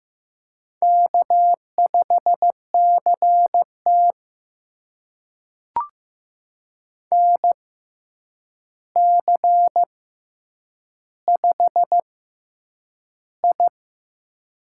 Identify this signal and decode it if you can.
Morse code, heard as a keyed tone, about 15 wpm, spaced as for 8 wpm, 700 Hz. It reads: K5CT NC5I
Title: USA Call Signs - 4 Characters - 6x Character Spacing - 15wpm